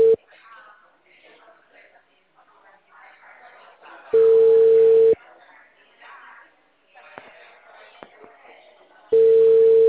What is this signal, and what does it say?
Ringback tone (China, 450 Hz), with speech.